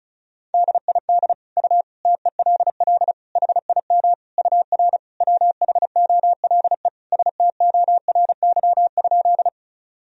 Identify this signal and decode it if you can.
Morse code, heard as a keyed tone, about 34 wpm, 700 Hz. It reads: DID U TELL HIM UR WHOLE STORY?
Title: Sentences from Top 400 Words and Abbreviations 35wpm